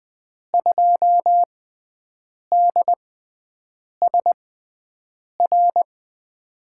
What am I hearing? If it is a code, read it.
Morse code, heard as a keyed tone, about 20 wpm, spaced as for 7 wpm, 700 Hz. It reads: 2DSR
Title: USA Call Signs - 6x Character Spacing - 20wpm